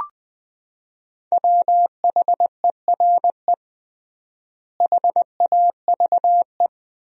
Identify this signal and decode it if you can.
Morse code, heard as a keyed tone, about 20 wpm, 700 Hz. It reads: WHERE HAVE
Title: 3x Word-Spacing - Sentences from Top 300 Words - 20wpm